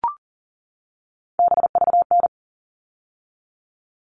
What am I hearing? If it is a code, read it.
Morse code, heard as a keyed tone, about 40 wpm, 700 Hz. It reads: BVN
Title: All Cut Numbers - 3-Digit 40wpm